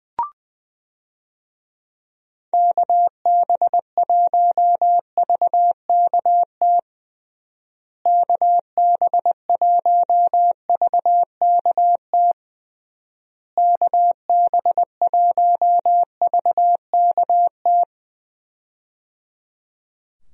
Morse code, heard as a keyed tone, about 20 wpm, 700 Hz. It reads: KB1VKT KB1VKT KB1VKT
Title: USA Call Signs - 6 Characters - Repeated 3x - 20wpm